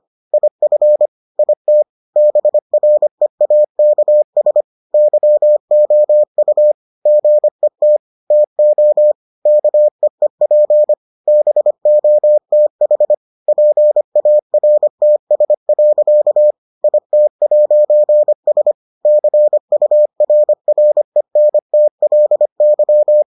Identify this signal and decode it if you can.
Morse code, heard as a keyed tone, about 25 wpm, 600 Hz. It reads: IF IT BREAKS YOU GET TO KEEP BOTH PARTS. IT'S CURRENTLY